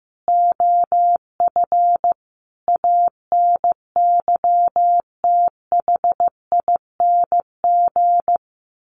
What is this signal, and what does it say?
Morse code, heard as a keyed tone, about 15 wpm, 700 Hz. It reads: OF ANYTHING
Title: Sentences from Top 1500 Words 15wpm